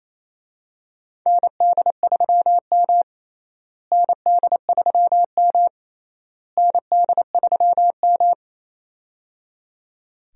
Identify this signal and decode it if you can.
Morse code, heard as a keyed tone, about 28 wpm, 700 Hz. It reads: ND3M ND3M ND3M
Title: USA Call Signs - 4 Characters - Repeated 3x - 28wpm